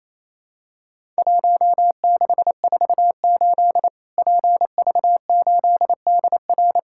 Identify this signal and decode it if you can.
Morse code, heard as a keyed tone, about 28 wpm, 700 Hz. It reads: 1648 PV8DR
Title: Worked All Europe DX Contest - QTC Exchange - 28wpm